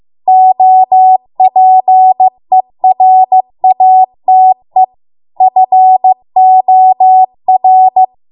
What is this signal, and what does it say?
Morse code, heard as a keyed tone, about 15 wpm, 750 Hz. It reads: OPERATE FOR